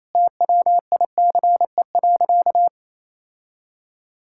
Morse code, heard as a keyed tone, about 28 wpm, 700 Hz. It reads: TWICE.
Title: Sentences from Top 1600 Words 28wpm